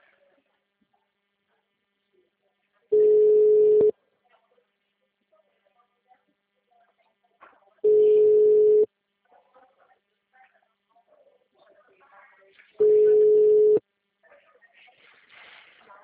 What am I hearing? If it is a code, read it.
Ringback tone (Europe, 425 Hz), with speech.